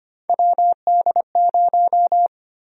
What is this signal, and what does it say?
Morse code, heard as a keyed tone, about 25 wpm, 700 Hz. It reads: WD0
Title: USA Call Sign Prefixes - Repeated 2x - 25wpm